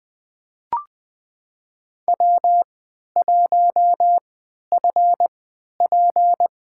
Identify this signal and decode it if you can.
Morse code, heard as a keyed tone, about 20 wpm, spaced as for 11 wpm, 700 Hz. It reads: W1FP
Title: USA Call Signs - 4 Characters - 3x Character Spacing - 20wpm